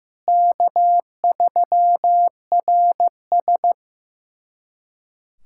Morse code, heard as a keyed tone, about 15 wpm, 700 Hz. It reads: K3RS
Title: USA Callsign DE Callsign - 4 Characters - 15wpm